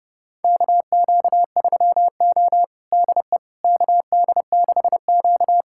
Morse code, heard as a keyed tone, about 30 wpm, 700 Hz. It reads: KQ3O DE KD6Q
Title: USA Callsign DE Callsign - 4 Characters - 30wpm